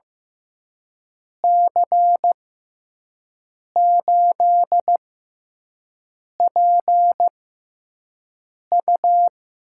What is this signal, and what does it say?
Morse code, heard as a keyed tone, about 15 wpm, spaced as for 5 wpm, 700 Hz. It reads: C8PU
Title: USA Call Signs - 6 Characters - 6x Character Spacing - 15wpm